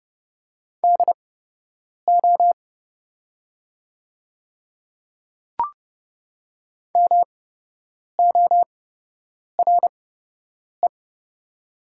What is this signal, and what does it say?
Morse code, heard as a keyed tone, about 30 wpm, spaced as for 8 wpm, 700 Hz. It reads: DO MORE
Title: 8x Character Spacing - Top 100 Words - 30wpm